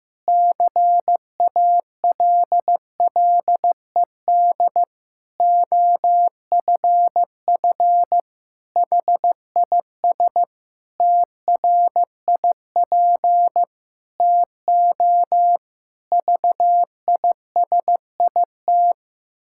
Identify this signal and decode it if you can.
Morse code, heard as a keyed tone, about 15 wpm, 700 Hz. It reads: CALLED OFF HIS TRIP TO VISIT
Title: Pangram Challenge v2 - 15wpm